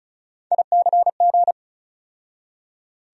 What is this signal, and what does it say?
Morse code, heard as a keyed tone, about 34 wpm, 700 Hz. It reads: ICG
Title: USA Call Sign Postfixes - 3 Letters 35wpm